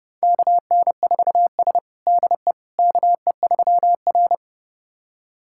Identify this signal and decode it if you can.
Morse code, heard as a keyed tone, about 32 wpm, 700 Hz. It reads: KN4S DE KE3R
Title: USA Callsign DE Callsign - 4 Characters - 30wpm